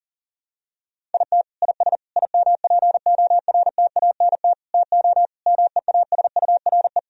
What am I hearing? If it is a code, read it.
Morse code, heard as a keyed tone, about 40 wpm, 700 Hz. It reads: IT IS IMPORTANT TO MEASURE
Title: Sentences from Top 1000 Words - Review Part 2 - 40wpm